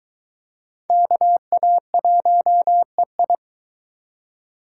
Morse code, heard as a keyed tone, about 20 wpm, 700 Hz. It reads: KA1EI